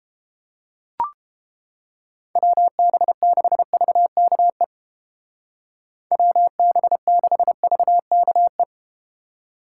Morse code, heard as a keyed tone, about 31 wpm, 700 Hz. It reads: WB6VKE WB6VKE